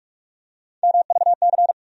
Morse code, heard as a keyed tone, about 44 wpm, 700 Hz. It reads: MVC
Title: USA Call Sign Postfixes - 3 Letters 45wpm